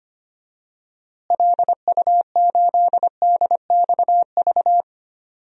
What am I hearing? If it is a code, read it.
Morse code, heard as a keyed tone, about 25 wpm, 700 Hz. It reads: LU8DXV
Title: DXCC-100 Call Signs - 6 Characters - Repeated 2x - 25wpm